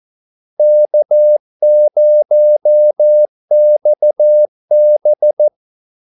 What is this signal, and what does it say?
Morse code, heard as a keyed tone, about 14 wpm, 600 Hz. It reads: K0XB
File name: Callsigns-14wpm